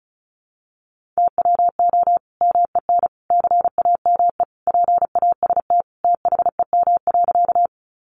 Morse code, heard as a keyed tone, about 34 wpm, 700 Hz. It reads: TWO MEN CAME PAST THEM.
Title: Sentences from Top 600 Words 35wpm